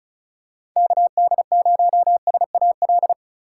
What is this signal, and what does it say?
Morse code, heard as a keyed tone, about 36 wpm, 700 Hz. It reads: KD0SAL
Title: USA Call Signs - 6 Characters - 4x Character Spacing - 35wpm